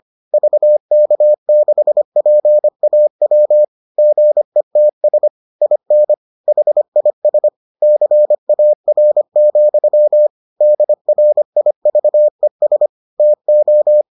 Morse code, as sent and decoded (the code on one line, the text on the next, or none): ...- -.- -.... .--. .- .-- / --. - ... / .. -. / .... .. ... / -.-. .- .-. --..-- / -.. .-. .. ...- . ... / - ---
VK6PAW GETS IN HIS CAR, DRIVES TO